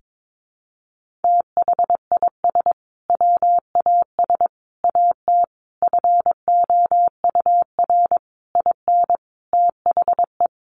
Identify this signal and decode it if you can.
Morse code, heard as a keyed tone, about 22 wpm, 700 Hz. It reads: THIS WAS AT FOUR IN THE
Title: Sentences from Top 1500 Words 22wpm